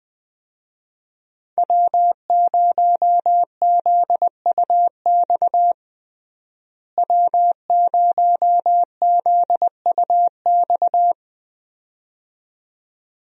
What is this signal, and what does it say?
Morse code, heard as a keyed tone, about 21 wpm, 700 Hz. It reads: W0ZUX W0ZUX
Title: USA Call Signs - 5 Characters - Repeated 2x - 20wpm